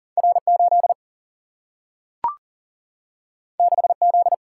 Morse code, heard as a keyed tone, about 41 wpm, 700 Hz. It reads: R8 6Z
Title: Single-Letter Single-Number 40wpm